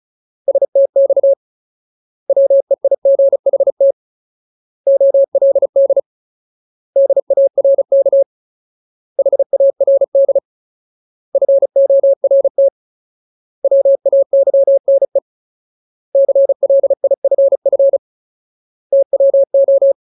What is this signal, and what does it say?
Morse code, heard as a keyed tone, about 34 wpm, 550 Hz. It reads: STX WEIGHT OLD DARK HARD FORT WAYNE CLIFF TWO
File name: Ultimate Word List / 35WPM Ultimate